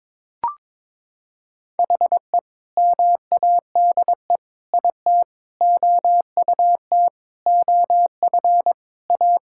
Morse code, heard as a keyed tone, about 22 wpm, 700 Hz. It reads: HE MADE IT OUT OF A
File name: practice / 22wpm / Sentences with 3-Word Phrases - 22wpm